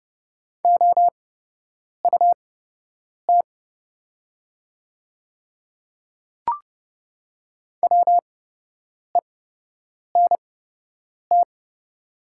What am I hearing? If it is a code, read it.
Morse code, heard as a keyed tone, about 30 wpm, spaced as for 8 wpm, 690 Hz. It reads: OUT WENT
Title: 8x Character Spacing - Top 200 Words - 30wpm